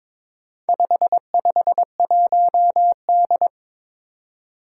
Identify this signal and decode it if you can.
Morse code, heard as a keyed tone, about 23 wpm, 700 Hz. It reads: 551D
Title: Four Challenging Letter-Number Combinations 22wpm